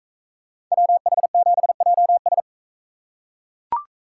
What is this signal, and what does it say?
Morse code, heard as a keyed tone, about 41 wpm, 700 Hz. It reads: WH7JS